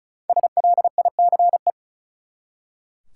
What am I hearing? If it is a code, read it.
Morse code, heard as a keyed tone, about 34 wpm, 700 Hz. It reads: SLICE
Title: Five Letter Words - 35wpm